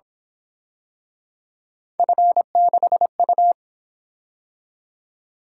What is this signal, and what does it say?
Morse code, heard as a keyed tone, about 26 wpm, 700 Hz. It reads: F6U
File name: Speed-Racing - Three Letter-Number Combinations - 22wpm